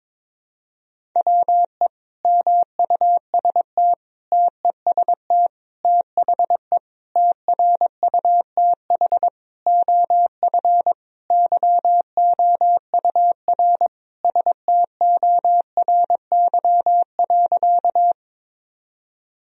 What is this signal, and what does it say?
Morse code, heard as a keyed tone, about 22 wpm, 700 Hz. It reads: WE MUST TEST THE TRUTH OF YOUR STORY.
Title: Sentences from Top 1000 Words - Review Part 3 - 22wpm